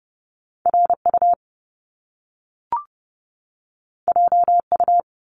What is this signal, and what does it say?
Morse code, heard as a keyed tone, about 30 wpm, 700 Hz. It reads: RU JU
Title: Top Two Letter Combinations in Words 30wpm